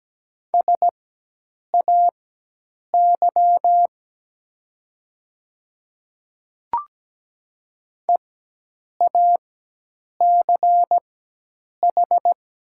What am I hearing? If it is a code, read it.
Morse code, heard as a keyed tone, about 17 wpm, spaced as for 8 wpm, 700 Hz. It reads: SAY EACH
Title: 4x Character Spacing - Top 200 Words - 17wpm